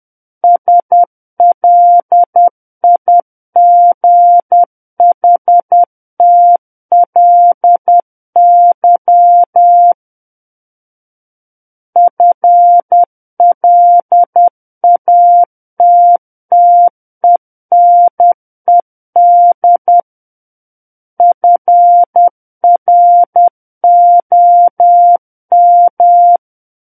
Morse code, heard as a keyed tone, about 10 wpm, 700 Hz. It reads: SLIGHTLY FLATTENED FROM
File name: Book One / 01-Chapter 1 - 10 WPM